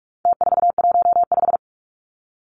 Morse code, heard as a keyed tone, about 44 wpm, 700 Hz. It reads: T415
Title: Numbers - Contest Style - 4-Digit 45wpm